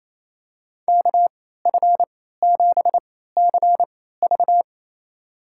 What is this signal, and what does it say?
Morse code, heard as a keyed tone, about 28 wpm, spaced as for 16 wpm, 700 Hz. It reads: KF7CV
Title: USA Call Signs - 3x Character Spacing - 28wpm